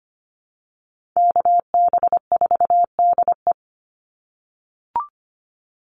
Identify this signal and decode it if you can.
Morse code, heard as a keyed tone, about 25 wpm, 700 Hz. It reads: KB4DE
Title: Call Signs - 25wpm